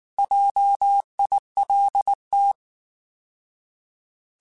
Morse code, heard as a keyed tone, about 19 wpm, 800 Hz. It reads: JILT
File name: Alphabetical / Alphabetical mp3s / Js @18